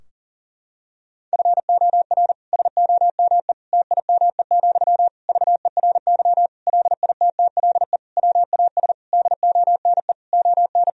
Morse code, heard as a keyed tone, about 40 wpm, 700 Hz. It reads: FOR SOME TIME, VERY LITTLE WAS DONE ON
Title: Sentences from Top 1000 Words - Review Part 4 - 40wpm